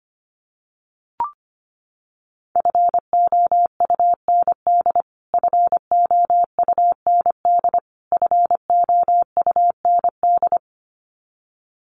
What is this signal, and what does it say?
Morse code, heard as a keyed tone, about 25 wpm, 700 Hz. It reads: FOUND FOUND FOUND
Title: Top 300 Words Repeated 25wpm